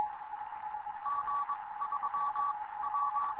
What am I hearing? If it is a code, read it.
Morse code, heard as a keyed tone, about 22 wpm, 1140 Hz. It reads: G 3 W
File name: Morze-HA2MN / hangp / 2 11 GYAKORLAT